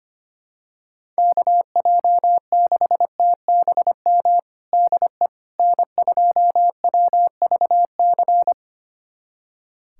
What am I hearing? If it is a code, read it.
Morse code, heard as a keyed tone, about 25 wpm, 700 Hz. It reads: KJ6TBM DE N2WVC
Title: USA Callsign DE Callsign 25wpm